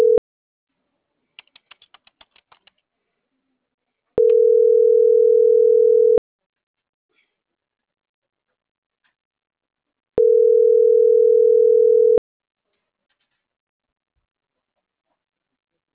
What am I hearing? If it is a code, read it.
Ringback tone (North America), 440 + 480 Hz.